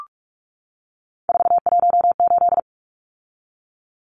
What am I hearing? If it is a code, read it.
Morse code, heard as a keyed tone, about 44 wpm, 700 Hz. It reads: 418
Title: Numbers - 3-Digit 45wpm